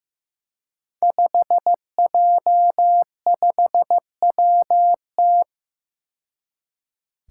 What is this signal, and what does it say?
Morse code, heard as a keyed tone, about 15 wpm, 700 Hz. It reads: 5J5WT
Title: Five Challenging Letter-Number Combinations 15wpm